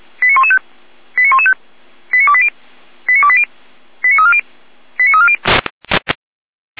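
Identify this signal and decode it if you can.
Five-tone selective call (ZVEI), heard as quick runs of five tones, tones of about 70 ms. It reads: 89115 89115 89280 89280 89340 89340
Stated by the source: ZVEI five-tone call 89262, received on SDR.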